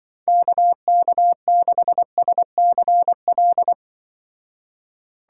Morse code, heard as a keyed tone, about 24 wpm, 700 Hz. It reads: KK6SCL
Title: Speed Racing - USA 2x3 Call Signs 20wpm